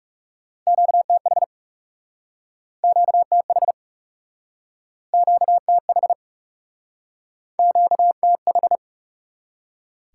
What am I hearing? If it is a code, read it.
Morse code, heard as a keyed tone, about 40 wpm, 700 Hz. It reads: QTH QTH QTH QTH